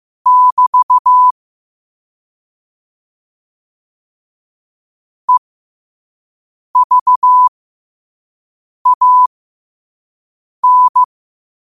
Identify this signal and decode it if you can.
Morse code, heard as a keyed tone, about 15 wpm, spaced as for 5 wpm, 1000 Hz. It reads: <BT> EVAN
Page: Quotes for Tue, 12 Aug 2025 in Morse Code at 5 words per minute.